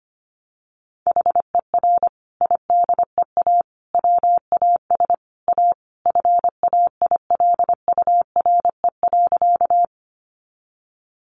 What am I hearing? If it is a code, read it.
Morse code, heard as a keyed tone, about 25 wpm, 700 Hz. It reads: HER IDEA WAS A FAILURE.
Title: Sentences from Top 900 Words 25wpm